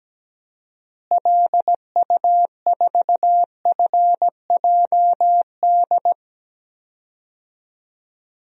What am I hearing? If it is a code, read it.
Morse code, heard as a keyed tone, about 17 wpm, 700 Hz. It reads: LU4FJD
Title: DXCC-100 Call Signs - 6 Characters - Repeated 3x - 17wpm